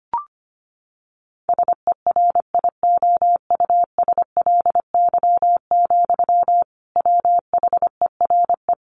Morse code, heard as a keyed tone, about 25 wpm, 700 Hz. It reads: SERIOUSLY, WHERE
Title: Sentences from Top 1200 Words 25wpm